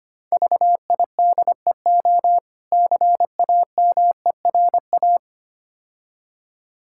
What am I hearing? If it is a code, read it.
Morse code, heard as a keyed tone, about 26 wpm, 700 Hz. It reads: VIDEO CAMERA
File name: practice / 25wpm / Difficult Words - Speed-Racing with Hints - 25wpm